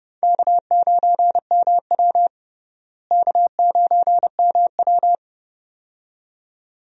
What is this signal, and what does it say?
Morse code, heard as a keyed tone, about 30 wpm, 700 Hz. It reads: K9MW K9MW
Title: USA Call Signs - 4 Characters - Repeated 2x - 30wpm